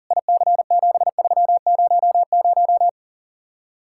Morse code, heard as a keyed tone, about 48 wpm, 700 Hz. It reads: IC7300
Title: Speed-Racing Common QSO Phrases - 40wpm